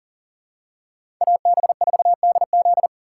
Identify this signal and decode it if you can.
Morse code, heard as a keyed tone, about 40 wpm, 700 Hz. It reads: AB4DZ
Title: USA Call Signs - 5 Characters - 6x Character Spacing - 40wpm